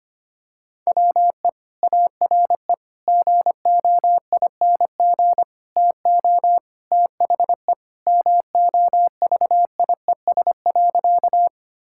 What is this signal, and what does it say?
Morse code, heard as a keyed tone, about 25 wpm, 700 Hz. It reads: WE ARE GOING TO THE MOVIES.
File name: practice / 25wpm / Sentences with 4-Word Phrases - 25wpm